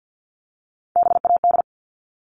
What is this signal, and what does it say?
Morse code, heard as a keyed tone, about 48 wpm, 700 Hz. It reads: BAD
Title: Top 501-750 Words 50wpm